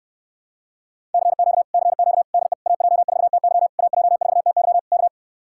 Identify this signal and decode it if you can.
Morse code, heard as a keyed tone, about 69 wpm, 700 Hz. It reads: CQ CQ DE AJ4TY AJ4TY K